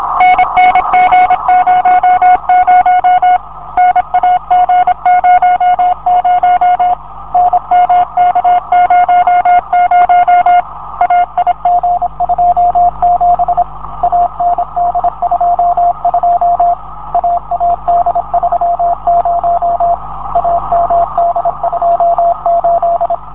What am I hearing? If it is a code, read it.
Morse code, heard as a keyed tone, about 26 wpm, 660 Hz, with radio noise. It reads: NNG00 NAG00 NMK00 AIG27 AND22 AAD30 AMD28